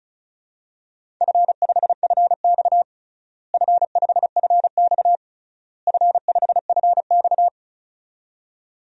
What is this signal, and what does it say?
Morse code, heard as a keyed tone, about 36 wpm, 700 Hz. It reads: F5FX F5FX F5FX
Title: DXCC-100 Call Signs - 4 Characters - Repeated 3x - 35wpm